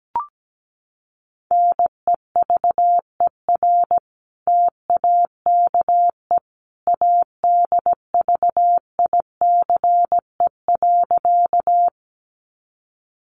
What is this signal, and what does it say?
Morse code, heard as a keyed tone, about 17 wpm, 700 Hz. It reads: NEVER TAKE ADVICE.
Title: Sentences from Top 1100 Words 17wpm